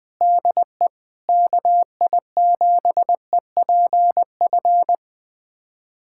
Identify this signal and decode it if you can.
Morse code, heard as a keyed tone, about 20 wpm, 700 Hz. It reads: DE KI7EPF
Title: USA Callsign DE Callsign - 6 Characters - 20wpm